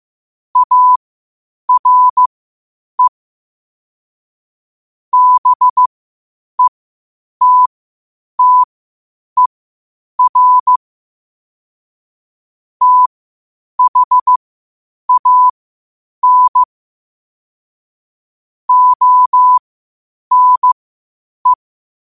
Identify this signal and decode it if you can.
Morse code, heard as a keyed tone, about 15 wpm, spaced as for 8 wpm, 1000 Hz. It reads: ARE BETTER THAN ONE
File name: 8wpm